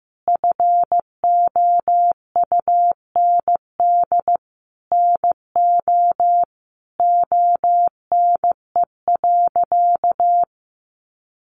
Morse code, heard as a keyed tone, about 15 wpm, 700 Hz. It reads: FOUND NO ONE.
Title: Sentences from Top 1400 Words 15wpm